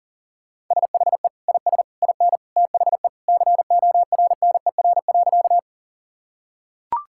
Morse code, heard as a keyed tone, about 40 wpm, 700 Hz. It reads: SHE IS IN THE CORNER.
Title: Sentences from Top 1600 Words 40wpm